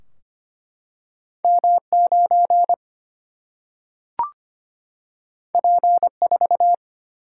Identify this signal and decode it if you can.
Morse code, heard as a keyed tone, about 25 wpm, 700 Hz. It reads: M 9 P 4
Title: Single-Letter Single-Number 25wpm